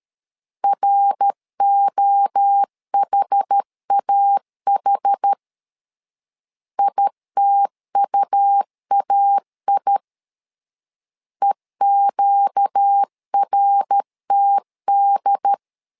Morse code, heard as a keyed tone, about 13 wpm, 800 Hz. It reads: ROHAH ITUAI EQRTD
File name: LEZIONI / 4 Q 12wpm 1833